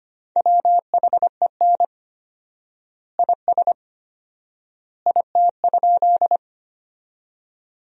Morse code, heard as a keyed tone, about 25 wpm, 700 Hz. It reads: WHEN IS IT?
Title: 4x Word-Spacing - Sentences from Top 100 Words - 25wpm